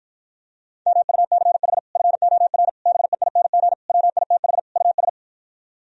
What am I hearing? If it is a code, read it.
Morse code, heard as a keyed tone, about 52 wpm, 700 Hz. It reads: MUCH FOR BEING WITH US